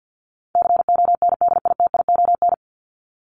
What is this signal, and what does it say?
Morse code, heard as a keyed tone, about 50 wpm, 700 Hz. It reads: CONDITION
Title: Top 751-1000 Words 50wpm